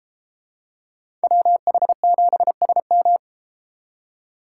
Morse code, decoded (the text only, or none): WH7SM